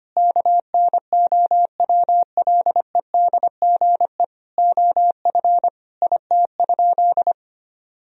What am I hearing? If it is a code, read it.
Morse code, heard as a keyed tone, about 25 wpm, 700 Hz. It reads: KNOWLEDGE OF IT?
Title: Sentences from Top 1000 Words - Review Part 3 - 25wpm